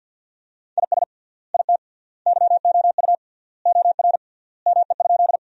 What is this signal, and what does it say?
Morse code, heard as a keyed tone, about 50 wpm, 700 Hz. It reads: IS IT YOU OR ME?
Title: 3x Word-Spacing - Sentences from Top 200 Words - 50wpm